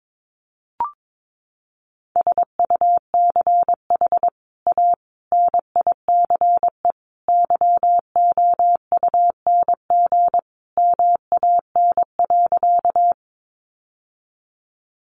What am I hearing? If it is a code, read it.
Morse code, heard as a keyed tone, about 22 wpm, 700 Hz. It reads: SUCH A NICE YOUNG MAN.
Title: Sentences from Top 1100 Words 22wpm